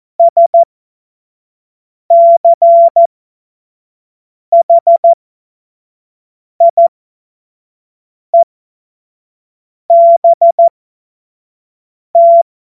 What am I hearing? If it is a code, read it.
Morse code, heard as a keyed tone, about 14 wpm, spaced as for 5 wpm, 680 Hz. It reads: SCHIEBT
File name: Morsekurs / Witze / Witz 01. Juni 2021, 25 F70